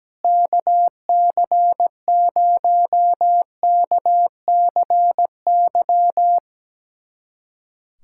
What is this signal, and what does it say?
Morse code, heard as a keyed tone, about 17 wpm, 700 Hz. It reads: KC0KCY
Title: USA Callsign DE Callsign 17wpm